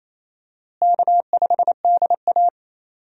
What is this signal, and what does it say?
Morse code, heard as a keyed tone, about 28 wpm, 700 Hz. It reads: K5DA
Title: USA Call Signs - 4 Characters - Repeated 2x - 28wpm